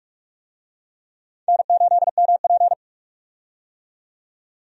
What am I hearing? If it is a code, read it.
Morse code, heard as a keyed tone, about 46 wpm, 700 Hz. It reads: N8MP
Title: USA Call Signs - 4 Characters - 45wpm